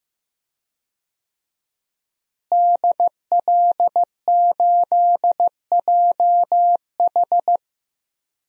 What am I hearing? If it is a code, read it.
Morse code, heard as a keyed tone, about 15 wpm, 700 Hz. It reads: DL8JH
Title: DXCC-100 Call Signs - 5 Characters - Repeated 3x - 15wpm